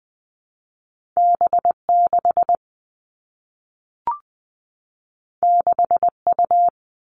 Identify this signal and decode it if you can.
Morse code, heard as a keyed tone, about 20 wpm, 700 Hz. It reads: B 6 6 U
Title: Two Challenging Letter-Number Combinations 20wpm